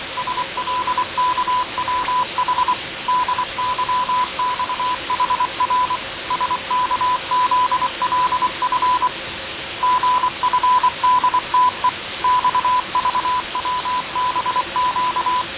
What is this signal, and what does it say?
Morse code, heard as a keyed tone, about 23 wpm, 1000 Hz. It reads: SLKWH DYXHR SKZLF GFDTE XVWBQ
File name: 6824usb